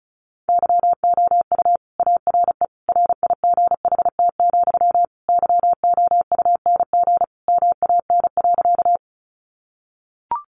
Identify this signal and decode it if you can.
Morse code, heard as a keyed tone, about 34 wpm, 700 Hz. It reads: YOU ARE RIGHT, YOUNG MAN.
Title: Sentences from Top 400 Words 35wpm